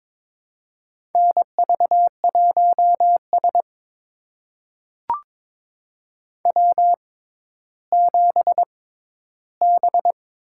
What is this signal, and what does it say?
Morse code, heard as a keyed tone, about 22 wpm, spaced as for 13 wpm, 700 Hz. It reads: NV1S W7B
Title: USA Call Signs - 4 Characters - 6x Character Spacing - 22wpm